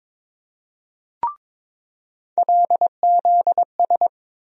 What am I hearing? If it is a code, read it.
Morse code, heard as a keyed tone, about 22 wpm, 700 Hz. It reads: LZS
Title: USA Call Sign Postfixes - 3 Letters 22wpm